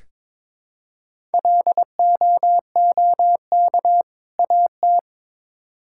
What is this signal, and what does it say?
Morse code, heard as a keyed tone, about 22 wpm, 700 Hz. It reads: LOOK AT
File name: Build A Phrase - 2-Word Phrases - 22wpm